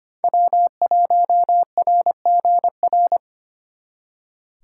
Morse code, heard as a keyed tone, about 33 wpm, 700 Hz. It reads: W1RGR